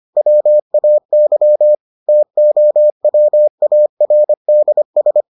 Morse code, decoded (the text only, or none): WAY TOWARDS